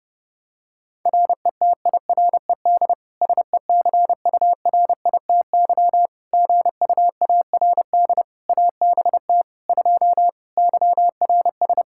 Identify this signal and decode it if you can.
Morse code, heard as a keyed tone, about 30 wpm, 700 Hz. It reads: RETIRED SECURITY GUARD ABT 2 YRS
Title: Practice QSO - Exchange 3 - More Info - 30wpm